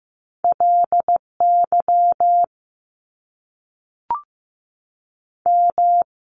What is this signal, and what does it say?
Morse code, heard as a keyed tone, about 15 wpm, 700 Hz. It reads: LY M
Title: Top Two Letter Combinations in Words 15wpm